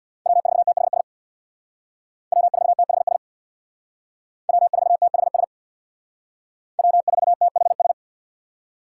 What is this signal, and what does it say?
Morse code, heard as a keyed tone, about 64 wpm, 700 Hz. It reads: W4THS W4THS W4THS W4THS